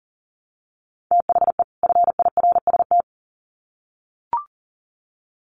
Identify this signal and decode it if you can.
Morse code, heard as a keyed tone, about 40 wpm, 700 Hz. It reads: THE FIRST
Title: Sets of 2 Words 40wpm